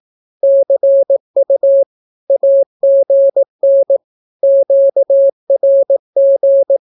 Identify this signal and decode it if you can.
Morse code, heard as a keyed tone, about 18 wpm, 550 Hz. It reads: CU AGN QRG
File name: naukapodstaw / skroty / skroty